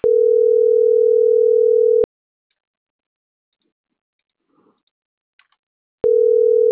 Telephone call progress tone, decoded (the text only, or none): ringback tone